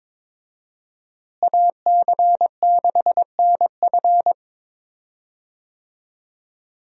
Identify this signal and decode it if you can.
Morse code, heard as a keyed tone, about 22 wpm, 700 Hz. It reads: AC6NF
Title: USA Call Signs - 5 Characters - Repeated 2x - 22wpm